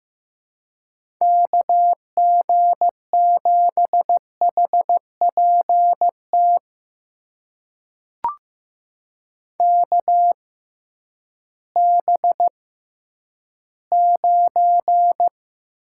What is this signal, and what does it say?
Morse code, heard as a keyed tone, about 15 wpm, spaced as for 10 wpm, 700 Hz. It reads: KG7HPT KB9
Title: USA Call Signs - 6x Character Spacing - 15wpm